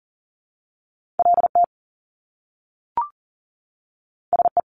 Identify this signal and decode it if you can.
Morse code, heard as a keyed tone, about 40 wpm, 700 Hz. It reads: LT SE
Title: Top Two Letter Combinations in Words 40wpm